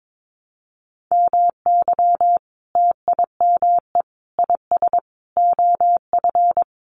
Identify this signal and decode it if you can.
Morse code, heard as a keyed tone, about 22 wpm, 700 Hz. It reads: MY TIME IS OF
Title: Sentences from Top 600 Words 22wpm